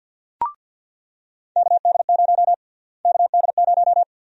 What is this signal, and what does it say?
Morse code, heard as a keyed tone, about 50 wpm, 700 Hz. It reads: KD0 KD0
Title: USA Call Sign Prefixes - Repeated 2x - 50wpm